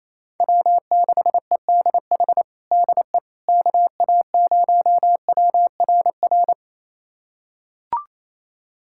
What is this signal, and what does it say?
Morse code, heard as a keyed tone, about 28 wpm, 700 Hz. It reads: W6EDH DE KA0WRR
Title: USA Callsign DE Callsign 28wpm